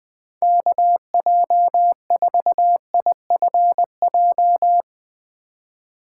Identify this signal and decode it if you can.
Morse code, heard as a keyed tone, about 20 wpm, 700 Hz. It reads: KJ4IFJ